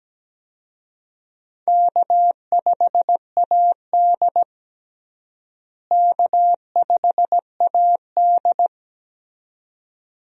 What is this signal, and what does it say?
Morse code, heard as a keyed tone, about 17 wpm, 700 Hz. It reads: K5AD K5AD
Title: USA Call Signs - 4 Characters - Repeated 2x - 17wpm